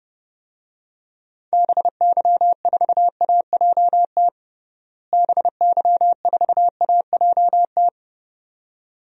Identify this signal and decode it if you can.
Morse code, heard as a keyed tone, about 29 wpm, 700 Hz. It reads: BY4AJT BY4AJT
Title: DXCC-100 Call Signs - 6 Characters - Repeated 2x - 30wpm